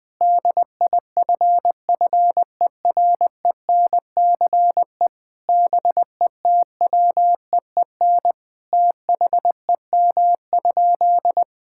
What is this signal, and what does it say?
Morse code, heard as a keyed tone, about 20 wpm, 700 Hz. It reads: DIFFERENCE BETWEEN THEM?
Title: Sentences from Top 1000 Words - Review Part 5 - 20wpm